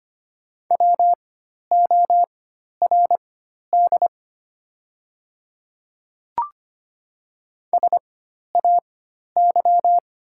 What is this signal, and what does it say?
Morse code, heard as a keyed tone, about 25 wpm, spaced as for 12 wpm, 690 Hz. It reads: WORD SAY
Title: 4x Character Spacing - Top 300 Words - 25wpm